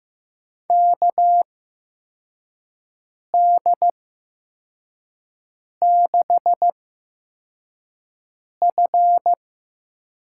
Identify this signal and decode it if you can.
Morse code, heard as a keyed tone, about 15 wpm, spaced as for 4 wpm, 700 Hz. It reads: KD6F
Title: USA Call Signs - 8x Character Spacing - 15wpm